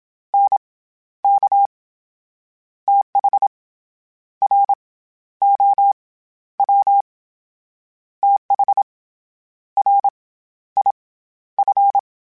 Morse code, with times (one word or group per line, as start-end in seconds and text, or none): 0.3-0.6 N
1.2-1.7 K
2.9-3.5 TH
4.4-4.7 R
5.4-5.9 O
6.6-7.0 W
8.2-8.8 TH
9.8-10.1 R
10.8-10.9 I
11.6-12.0 F